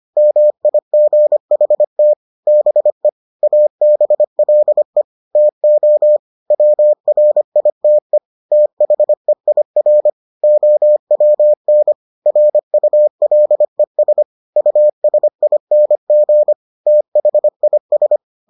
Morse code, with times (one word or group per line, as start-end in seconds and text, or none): 0.2-2.1 MIGHT
2.5-3.1 BE
3.4-5.0 ABLE
5.4-6.2 TO
6.5-8.2 WRITE
8.5-10.1 THEIR
10.4-11.9 OWN
12.3-14.2 RULES
14.6-16.5 USING
16.9-18.2 THIS